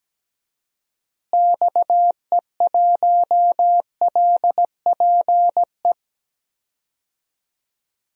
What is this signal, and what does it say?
Morse code, heard as a keyed tone, about 17 wpm, 700 Hz. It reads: XE1LPE
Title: DXCC-100 Call Signs - 6 Characters - Repeated 3x - 17wpm